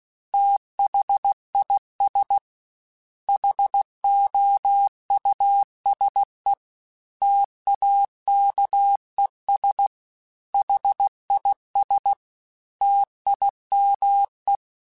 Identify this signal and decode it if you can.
Morse code, heard as a keyed tone, about 16 wpm, 800 Hz. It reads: THIS HOUSE TAKES HIS TIME
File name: T then H Runs / T then H Runs.mp3 / T then H Runs @15